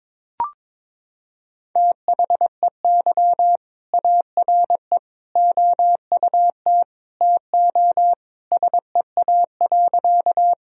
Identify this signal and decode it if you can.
Morse code, heard as a keyed tone, about 21 wpm, 700 Hz. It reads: THEY ARE OUT TO SEA.
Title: Sentences from Top 1000 Words - Review Part 1 - 22wpm